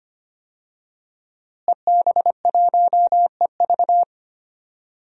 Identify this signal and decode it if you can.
Morse code, heard as a keyed tone, about 25 wpm, 700 Hz. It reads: EB1EV
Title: DXCC-100 Call Signs - 5 Characters - Repeated 2x - 25wpm